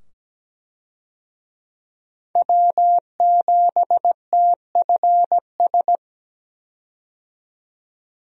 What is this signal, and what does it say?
Morse code, heard as a keyed tone, about 17 wpm, 700 Hz. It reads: W7TFS
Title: USA Call Signs - 5 Characters - 17wpm